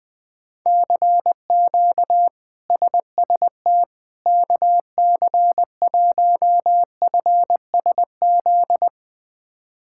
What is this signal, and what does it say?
Morse code, heard as a keyed tone, about 21 wpm, 700 Hz. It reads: CQ SST KC1FSZ
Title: K1USN SST QSO Practice - 20wpm